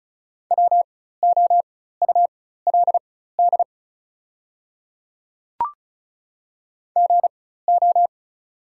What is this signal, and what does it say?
Morse code, heard as a keyed tone, about 36 wpm, spaced as for 16 wpm, 690 Hz. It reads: WOULD GO
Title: 4x Character Spacing - Top 100 Words - 35wpm